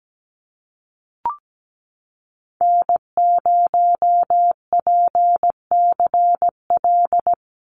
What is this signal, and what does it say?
Morse code, heard as a keyed tone, about 17 wpm, 700 Hz. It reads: N0PCL
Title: NA SOTA Activator Callsigns 17wpm